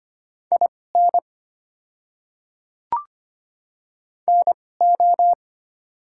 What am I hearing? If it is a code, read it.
Morse code, heard as a keyed tone, about 25 wpm, spaced as for 18 wpm, 700 Hz. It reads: IN NO
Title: 2x Character Spacing - Top 200 Words - 25wpm